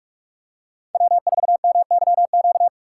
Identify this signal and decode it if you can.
Morse code, heard as a keyed tone, about 48 wpm, 700 Hz. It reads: W4MYQ